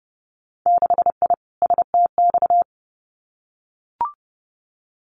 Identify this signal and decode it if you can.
Morse code, heard as a keyed tone, about 30 wpm, 700 Hz. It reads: 6I STX
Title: Winter Field Day Exchanges Encore 30wpm